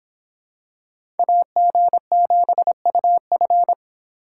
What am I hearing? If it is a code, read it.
Morse code, heard as a keyed tone, about 25 wpm, 700 Hz. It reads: AG7UF